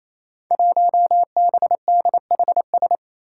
Morse code, heard as a keyed tone, about 28 wpm, 700 Hz. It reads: 1BDHS
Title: Five Challenging Letter-Number Combinations 28wpm